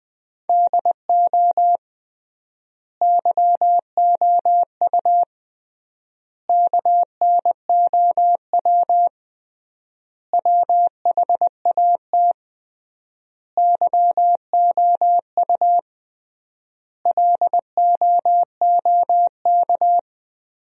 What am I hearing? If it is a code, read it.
Morse code, heard as a keyed tone, about 21 wpm, 700 Hz. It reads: DO YOU KNOW WHAT YOU LOOK
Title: 3x Word-Spacing - Sentences from Top 100 Words - 20wpm